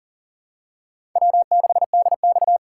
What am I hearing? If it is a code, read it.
Morse code, heard as a keyed tone, about 43 wpm, 690 Hz. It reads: W6DX
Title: USA Call Signs - 4 Characters - 2x Character Spacing - 40wpm